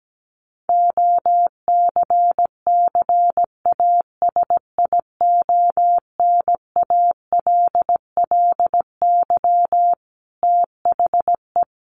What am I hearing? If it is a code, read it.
Morse code, heard as a keyed tone, about 17 wpm, 700 Hz. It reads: OCCASIONALLY THE
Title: Sentences from Top 1300 Words 17wpm